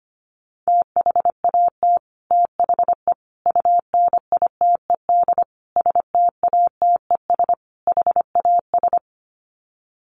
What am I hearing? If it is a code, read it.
Morse code, heard as a keyed tone, about 25 wpm, 700 Hz. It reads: THAT THE UNITED STATES HAS
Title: Sets of 5 Words Encore 25wpm